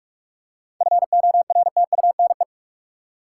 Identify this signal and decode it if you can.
Morse code, heard as a keyed tone, about 47 wpm, 700 Hz. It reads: FORTUNE
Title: Seven Letter Words - 45wpm